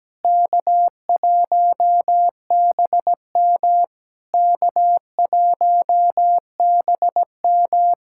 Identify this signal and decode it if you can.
Morse code, heard as a keyed tone, about 17 wpm, 700 Hz. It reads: K1BM K1BM
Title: SOTA QSO Practice 17wpm